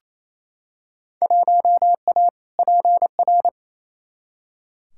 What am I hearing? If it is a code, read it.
Morse code, heard as a keyed tone, about 28 wpm, 700 Hz. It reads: 1A PR
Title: Field Day Exchanges Encore - 28wpm